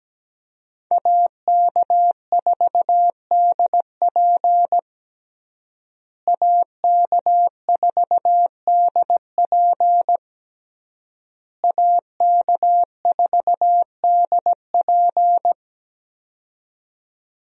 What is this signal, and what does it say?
Morse code, heard as a keyed tone, about 17 wpm, 700 Hz. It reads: AK4DP AK4DP AK4DP
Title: USA Call Signs - 5 Characters - Repeated 3x - 17wpm